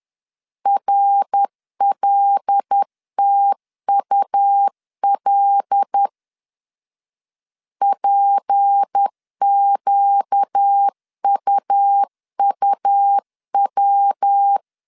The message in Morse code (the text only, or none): RLTUL PQUUW